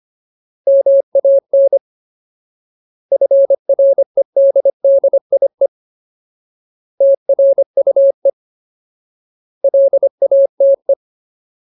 Morse code, heard as a keyed tone, about 25 wpm, 550 Hz. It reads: MAN FREDDIE TRUE LATE
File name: Ultimate Word List / 25WPM Ultimate